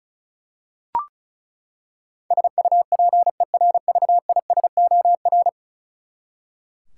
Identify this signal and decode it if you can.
Morse code, heard as a keyed tone, about 36 wpm, 700 Hz. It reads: SUPERVISOR
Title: Practice QSO - Exchange 3 - More Info - 35wpm